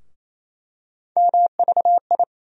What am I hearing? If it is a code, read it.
Morse code, heard as a keyed tone, about 28 wpm, 700 Hz. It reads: MVI
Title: USA Call Sign Postfixes - 3 Letters 28wpm